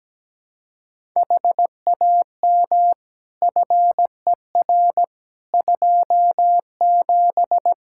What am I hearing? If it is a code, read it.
Morse code, heard as a keyed tone, about 17 wpm, 700 Hz. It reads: HAM FER 27
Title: Practice QSO - Exchange 3 - More Info - 17wpm